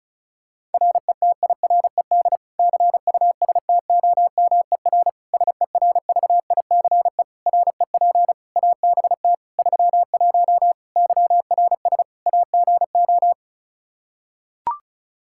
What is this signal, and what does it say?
Morse code, heard as a keyed tone, about 36 wpm, 700 Hz. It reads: RETIRED CUSTOMER SERVICE REP ABT 31 YRS AGO
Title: Practice QSO - Exchange 3 - More Info - 35wpm